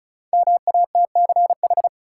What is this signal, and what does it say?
Morse code, heard as a keyed tone, about 36 wpm, 700 Hz. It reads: MATCH
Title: Sentences from Top 1600 Words - New Word Drill - 35wpm